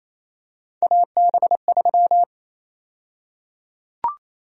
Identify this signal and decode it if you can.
Morse code, heard as a keyed tone, about 27 wpm, 700 Hz. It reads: AB3
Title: USA Call Sign Prefixes - 28wpm